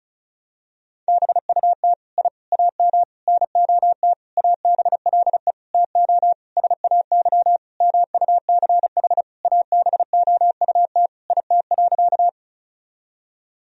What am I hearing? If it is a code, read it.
Morse code, heard as a keyed tone, about 36 wpm, 700 Hz. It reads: BUT I AM NOT ABLE TO SAY MUCH ABOUT IT.
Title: Sentences from Top 1000 Words - Review Part 2 - 35wpm